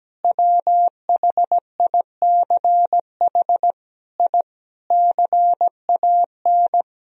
Morse code, heard as a keyed tone, about 17 wpm, 700 Hz. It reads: WHICH I CAN
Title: Sentences from Top 1000 Words - Review Part 2 - 17wpm